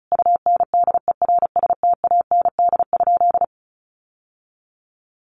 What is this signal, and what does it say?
Morse code, heard as a keyed tone, about 34 wpm, 700 Hz. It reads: UNDERSTAND?
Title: Sentences from Top 600 Words 35wpm